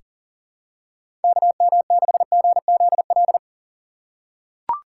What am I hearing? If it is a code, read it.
Morse code, heard as a keyed tone, about 38 wpm, 700 Hz. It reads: KM6GZL
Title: USA Call Signs - 6 Characters - 3x Character Spacing - 40wpm